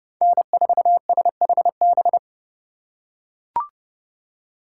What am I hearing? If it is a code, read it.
Morse code, heard as a keyed tone, about 32 wpm, 700 Hz. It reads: N4SHB
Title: USA Call Signs - 5 Characters - 4x Character Spacing - 30wpm